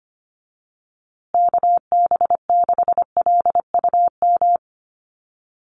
Morse code, heard as a keyed tone, about 25 wpm, 700 Hz. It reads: KB6LUM
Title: USA Call Signs Repeated 2x 25wpm